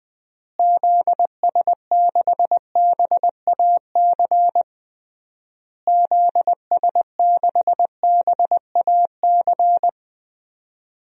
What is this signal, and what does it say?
Morse code, heard as a keyed tone, about 20 wpm, 700 Hz. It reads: ZS6BAC ZS6BAC
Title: DXCC-100 Call Signs - 6 Characters - Repeated 3x - 20wpm